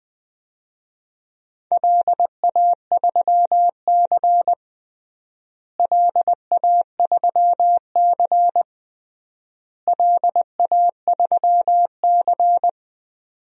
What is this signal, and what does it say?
Morse code, heard as a keyed tone, about 20 wpm, 700 Hz. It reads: LA3C LA3C LA3C
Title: DXCC-100 Call Signs - 4 Characters - Repeated 3x - 20wpm